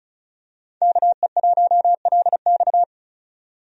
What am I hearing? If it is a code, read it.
Morse code, heard as a keyed tone, about 36 wpm, 700 Hz. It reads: KE1LX
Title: USA Call Signs - 5 Characters - 35wpm